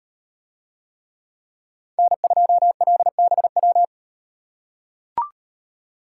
Morse code, heard as a keyed tone, about 38 wpm, 700 Hz. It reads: N2LBW